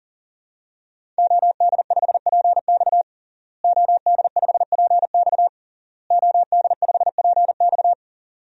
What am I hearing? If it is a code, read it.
Morse code, heard as a keyed tone, about 37 wpm, 700 Hz. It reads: OD5PX OD5PX OD5PX
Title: DXCC-100 Call Signs - 5 Characters - Repeated 3x - 40wpm